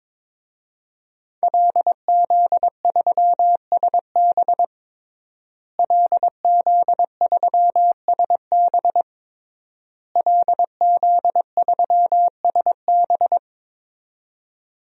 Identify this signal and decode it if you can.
Morse code, heard as a keyed tone, about 22 wpm, 700 Hz. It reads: LZ3SB LZ3SB LZ3SB
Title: DXCC-100 Call Signs - 5 Characters - Repeated 3x - 22wpm